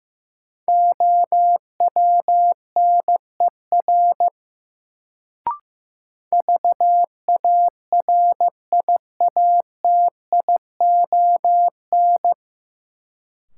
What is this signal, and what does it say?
Morse code, heard as a keyed tone, about 15 wpm, 700 Hz. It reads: OWNER VARIATION
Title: Sentences from Top 1600 Words - New Word Drill Alt - 15wpm